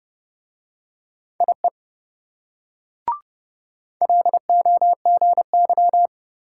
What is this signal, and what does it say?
Morse code, heard as a keyed tone, about 26 wpm, 700 Hz. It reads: IE LOGY